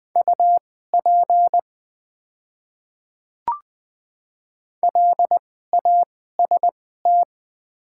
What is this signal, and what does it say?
Morse code, heard as a keyed tone, about 20 wpm, spaced as for 15 wpm, 690 Hz. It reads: UP LAST
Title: 2x Character Spacing - Top 500 Words - 20wpm